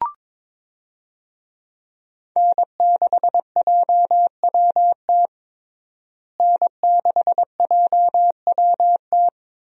Morse code, heard as a keyed tone, about 22 wpm, 700 Hz. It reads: N6JWT N6JWT
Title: USA Call Signs - 5 Characters - Repeated 2x - 22wpm